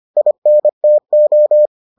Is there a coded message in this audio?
Yes: Morse code from 0.2 to 1.7 s, heard as a keyed tone, about 25 wpm, 600 Hz. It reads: INTO